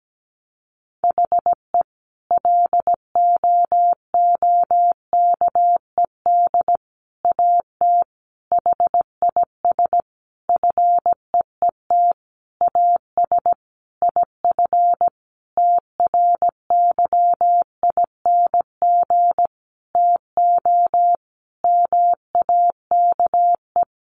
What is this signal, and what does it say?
Morse code, heard as a keyed tone, about 17 wpm, 700 Hz. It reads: HE LOOKED AT HIS FEET AS IF TRYING TO MAKE
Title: A Sound Of Thunder 17wpm